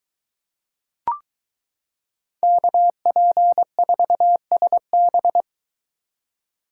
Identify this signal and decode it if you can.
Morse code, heard as a keyed tone, about 23 wpm, 700 Hz. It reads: KP4SB